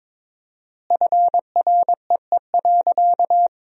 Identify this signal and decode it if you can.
Morse code, heard as a keyed tone, about 23 wpm, 700 Hz. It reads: FREE.
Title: 4x Word-Spacing - Sentences from Top 500 Words - 22wpm